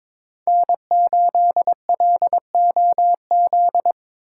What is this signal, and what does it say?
Morse code, heard as a keyed tone, about 26 wpm, 700 Hz. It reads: N8LOZ
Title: Speed Racing - USA 1x3 Call Signs 22wpm